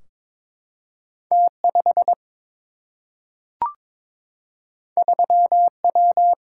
Morse code, heard as a keyed tone, about 21 wpm, 700 Hz. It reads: T 5 3 W
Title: Single-Letter Single-Number 22wpm